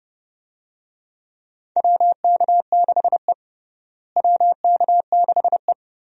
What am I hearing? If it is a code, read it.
Morse code, heard as a keyed tone, about 29 wpm, 700 Hz. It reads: WK6E WK6E
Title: USA Call Signs - 4 Characters - Repeated 3x - 30wpm